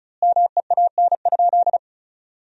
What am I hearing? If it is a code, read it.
Morse code, heard as a keyed tone, about 34 wpm, 700 Hz. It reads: MEAN?
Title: 3x Word-Spacing - Sentences from Top 200 Words - 35wpm